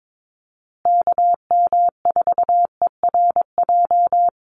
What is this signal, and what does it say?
Morse code, heard as a keyed tone, about 22 wpm, 700 Hz. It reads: KM4ERJ
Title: USA Call Signs Repeated 2x 22wpm